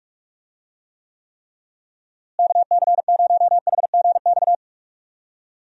Morse code, heard as a keyed tone, about 44 wpm, 700 Hz. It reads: KC0HGX
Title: USA Call Signs - 6 Characters - Repeated 3x - 45wpm